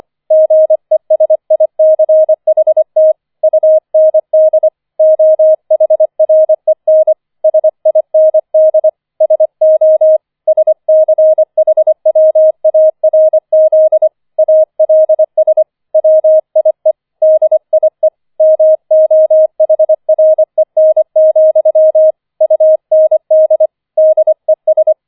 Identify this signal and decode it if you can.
Morse code, heard as a keyed tone, about 25 wpm, 630 Hz. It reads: GESICHT UND OHREN SIND SO SCHWARZ ALS WIE DIE MOHREN, UND DES
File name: 004 Vierter Streich 25wpm